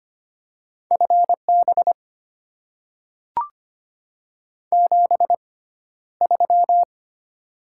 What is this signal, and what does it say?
Morse code, heard as a keyed tone, about 25 wpm, spaced as for 13 wpm, 690 Hz. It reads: FB 73
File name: Top 20 QSO Elements - 6x Character Spacing - 25wpm